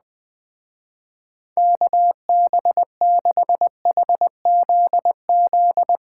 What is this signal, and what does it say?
Morse code, heard as a keyed tone, about 20 wpm, 700 Hz. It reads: KB6HZZ
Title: Speed Racing - USA 2x3 Call Signs 17wpm